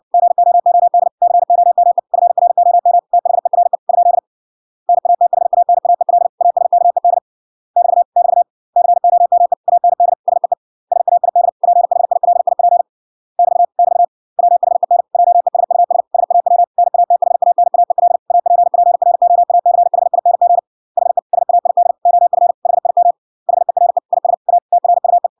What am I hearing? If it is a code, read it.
Morse code, heard as a keyed tone, about 58 wpm, 700 Hz. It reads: GOOD COME FROM THERE ? NATHANAEL ASKED <BT> <BT> COME AND SEE SAID PHILIP <BT> <BT> WHEN JESUS SAW NATHANAEL APPROACHING HE SAID OF HIM HERE IS A TRUE